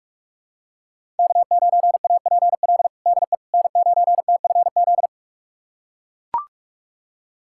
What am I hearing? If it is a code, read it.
Morse code, heard as a keyed tone, about 41 wpm, 700 Hz. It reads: K9APL DE N9TFZ
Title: USA Callsign DE Callsign - 5 Characters - 45wpm